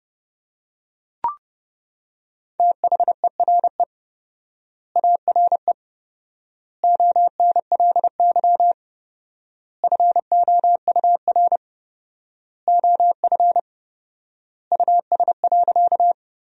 Morse code, heard as a keyed tone, about 30 wpm, 700 Hz. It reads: THERE ARE ONLY FOUR OF US.
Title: 4x Word-Spacing - Sentences from Top 300 Words - 30wpm